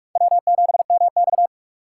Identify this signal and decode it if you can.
Morse code, heard as a keyed tone, about 44 wpm, 700 Hz. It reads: W7MX
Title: USA Call Signs - 4 Characters - 6x Character Spacing - 45wpm